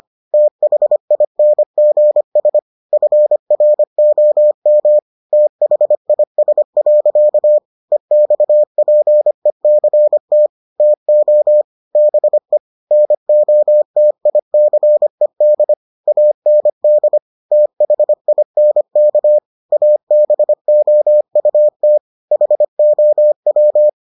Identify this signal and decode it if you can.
Morse code, heard as a keyed tone, about 25 wpm, 600 Hz. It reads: THINGS FROM THIS. EXPECT TO BE NOTICED AND THINK ABOUT HOW